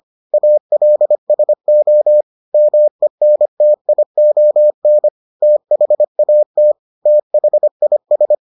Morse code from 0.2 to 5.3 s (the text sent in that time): ALSO MENTION